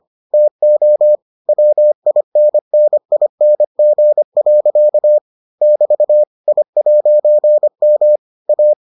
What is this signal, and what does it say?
Morse code, heard as a keyed tone, about 25 wpm, 600 Hz. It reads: TO WINNING. <BT> I'M A